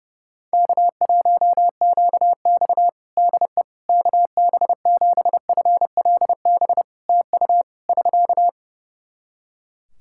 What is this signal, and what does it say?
Morse code, heard as a keyed tone, about 30 wpm, 700 Hz. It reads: K1QX DE KB7FLB TU <SK>
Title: Practice QSO - Exchange 4 - Ending - 30wpm